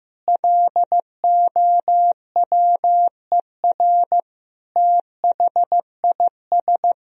Morse code, heard as a keyed tone, about 15 wpm, 700 Hz. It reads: LOWER THIS
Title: Sentences from Top 1000 Words - Review Part 4 - 15wpm